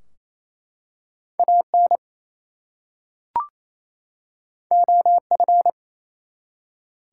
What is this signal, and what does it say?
Morse code, heard as a keyed tone, about 27 wpm, 700 Hz. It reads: AN OF
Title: Two Letter Words - 28wpm